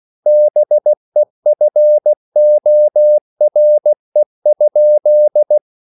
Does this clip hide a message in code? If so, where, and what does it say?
Yes: Morse code from 0.3 to 5.6 s, heard as a keyed tone, about 16 wpm, 600 Hz. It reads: BEFORE?